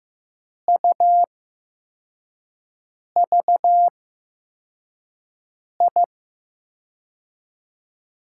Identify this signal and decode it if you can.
Morse code, heard as a keyed tone, about 15 wpm, spaced as for 4 wpm, 700 Hz. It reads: UVI
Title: USA Call Signs - 8x Character Spacing - 15wpm